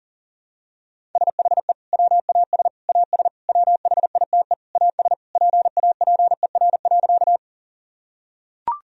Morse code, heard as a keyed tone, about 40 wpm, 700 Hz. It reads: SHE WAS AS WHITE AS PAPER.
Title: Sentences from Top 1000 Words - Review Part 1 - 40wpm